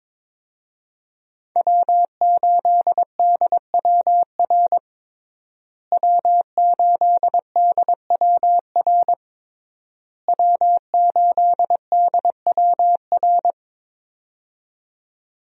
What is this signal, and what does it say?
Morse code, heard as a keyed tone, about 22 wpm, 700 Hz. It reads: W8DWR W8DWR W8DWR
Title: USA Call Signs - 5 Characters - Repeated 3x - 22wpm